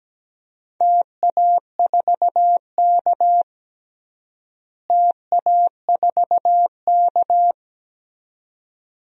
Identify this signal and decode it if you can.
Morse code, heard as a keyed tone, about 17 wpm, 700 Hz. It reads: TA4K TA4K
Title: DXCC-100 Call Signs - 4 Characters - Repeated 2x - 17wpm